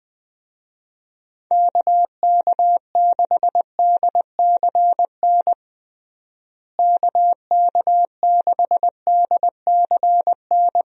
Morse code, heard as a keyed tone, about 20 wpm, 700 Hz. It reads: KK6DCN KK6DCN
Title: USA Call Signs - 6 Characters - Repeated 3x - 20wpm